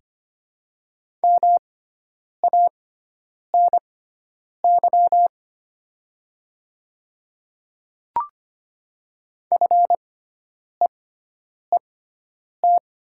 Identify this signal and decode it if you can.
Morse code, heard as a keyed tone, about 25 wpm, spaced as for 9 wpm, 700 Hz. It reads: MANY FEET
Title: 6x Character Spacing - Top 500 Words - 25wpm